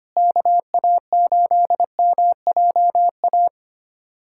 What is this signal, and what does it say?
Morse code, heard as a keyed tone, about 25 wpm, 700 Hz. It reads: KA8MJA
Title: USA Callsign DE Callsign 25wpm